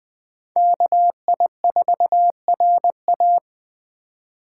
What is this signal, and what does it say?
Morse code, heard as a keyed tone, about 21 wpm, 700 Hz. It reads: KI4RA